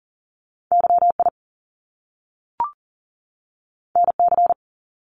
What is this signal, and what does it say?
Morse code, heard as a keyed tone, about 39 wpm, 700 Hz. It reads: YI NC
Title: Top Two Letter Combinations in Words 40wpm